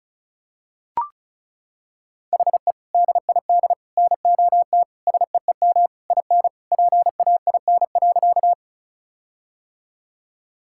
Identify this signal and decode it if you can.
Morse code, heard as a keyed tone, about 36 wpm, 700 Hz. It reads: HE DID NOT SEEM IN PAIN.
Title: Sentences from Top 1600 Words 35wpm